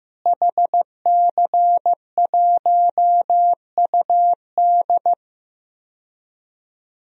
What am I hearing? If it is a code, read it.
Morse code, heard as a keyed tone, about 15 wpm, 700 Hz. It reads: HC1UD
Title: DXCC-100 Call Signs - 5 Characters - Repeated 3x - 15wpm